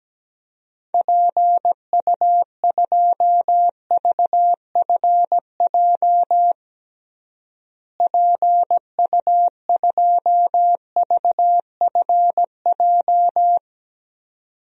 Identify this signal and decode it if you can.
Morse code, heard as a keyed tone, about 17 wpm, 700 Hz. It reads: PU2VFJ PU2VFJ
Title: DXCC-100 Call Signs - 6 Characters - Repeated 2x - 17wpm